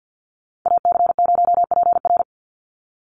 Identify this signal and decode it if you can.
Morse code, heard as a keyed tone, about 48 wpm, 700 Hz. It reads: AC0PR
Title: NA SOTA Chaser Callsigns 50wpm